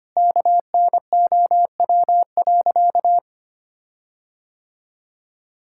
Morse code, heard as a keyed tone, about 25 wpm, 700 Hz. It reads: KNOW.
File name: practice / 25wpm / Sentences with 4-Word Phrases - 25wpm